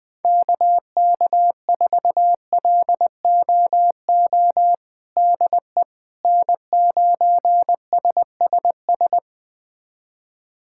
Morse code, heard as a keyed tone, about 20 wpm, 700 Hz. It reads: KK4LOO DE N9SSS
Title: USA Callsign DE Callsign 20wpm